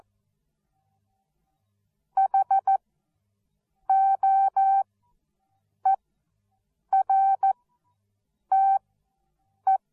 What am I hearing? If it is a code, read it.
Morse code, heard as a keyed tone, about 14 wpm, spaced as for 7 wpm, 780 Hz. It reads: HOERTE
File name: Onlinelehrgang / morsen / DJ4UF Morsen13